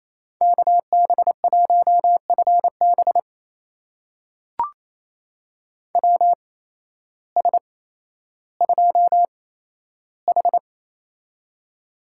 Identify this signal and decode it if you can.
Morse code, heard as a keyed tone, about 27 wpm, spaced as for 13 wpm, 700 Hz. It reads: KB1FB WS2H
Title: USA Call Signs - 8x Character Spacing - 28wpm